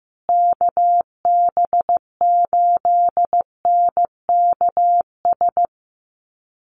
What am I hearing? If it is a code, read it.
Morse code, heard as a keyed tone, about 15 wpm, 700 Hz. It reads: KB8NKS
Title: USA Call Signs Repeated 3x 15wpm